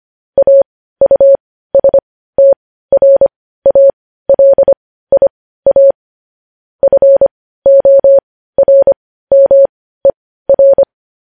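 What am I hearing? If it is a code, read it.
Morse code, heard as a keyed tone, about 25 wpm, spaced as for 15 wpm, 550 Hz. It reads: AUSTRALIA FORMER
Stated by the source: Morse news headlines sent at 15 words per minute.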